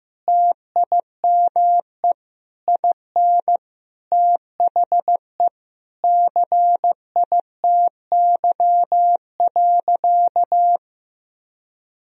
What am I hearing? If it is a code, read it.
Morse code, heard as a keyed tone, about 16 wpm, 700 Hz. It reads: TIME IN THE CITY.
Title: Sentences from Top 1000 Words - Review Part 4 - 15wpm